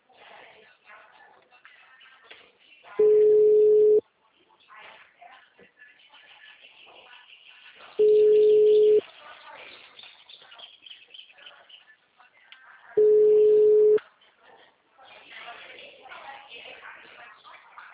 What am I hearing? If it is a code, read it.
Ringback tone (Europe, 425 Hz), with speech.